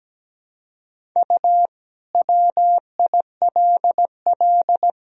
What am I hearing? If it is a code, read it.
Morse code, heard as a keyed tone, about 17 wpm, 700 Hz. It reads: U WILL
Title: Sentences from Top 400 Words and Abbreviations 17wpm